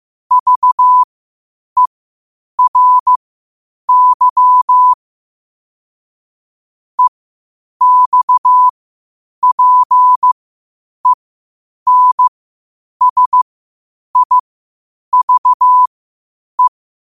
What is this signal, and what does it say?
Morse code, heard as a keyed tone, about 15 wpm, spaced as for 8 wpm, 1000 Hz. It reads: VERY EXPENSIVE
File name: qotdfiles / 8wpm